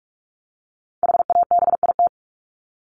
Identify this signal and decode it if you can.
Morse code, heard as a keyed tone, about 44 wpm, 700 Hz. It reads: HABIT
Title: Top 1751-2000 Words 45wpm